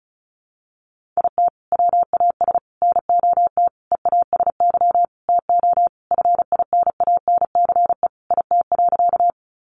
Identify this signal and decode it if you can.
Morse code, heard as a keyed tone, about 34 wpm, 700 Hz. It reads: IT WAS NOT EASY TO FINANCE IT.
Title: Sentences from Top 1500 Words 35wpm